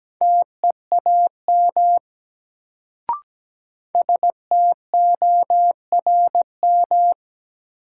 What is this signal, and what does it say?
Morse code, heard as a keyed tone, about 17 wpm, 700 Hz. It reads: TEAM STORM
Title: Sentences from Top 1500 Words - New Word Drill - 17wpm